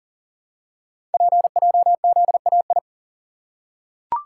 Morse code, heard as a keyed tone, about 38 wpm, 690 Hz. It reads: PJZAI
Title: Code Groups - 5 Letters - 40wpm